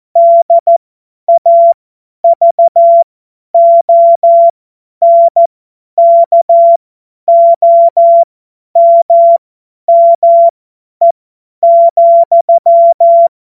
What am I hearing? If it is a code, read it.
Morse code, heard as a keyed tone, about 14 wpm, spaced as for 10 wpm, 680 Hz. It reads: DAVONKOMME,
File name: Morsekurs / Witze / Witz 03. Mai 2022 50 F70